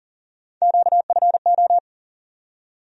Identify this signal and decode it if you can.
Morse code, heard as a keyed tone, about 40 wpm, 700 Hz. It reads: QFO
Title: USA Call Sign Postfixes - 3 Letters 40wpm